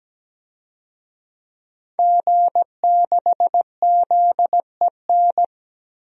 Morse code, heard as a keyed tone, about 17 wpm, 700 Hz. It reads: G6ZEN
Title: DXCC-100 Call Signs - 5 Characters - Repeated 3x - 17wpm